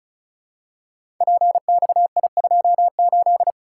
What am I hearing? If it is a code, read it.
Morse code, heard as a keyed tone, about 36 wpm, 700 Hz. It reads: PXI28
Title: Code Groups - 5 Letter:Numbers - 35wpm